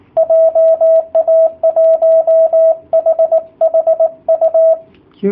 Morse code, heard as a keyed tone, about 18 wpm, 660 Hz, with radio noise. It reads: JA1HHU
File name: jh8cbh / 2019cw / sp18- w